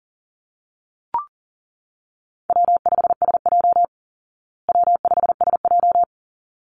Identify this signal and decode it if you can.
Morse code, heard as a keyed tone, about 40 wpm, 700 Hz. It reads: W5SJ W5SJ
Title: Easy Call Signs Repeated 2x - 40wpm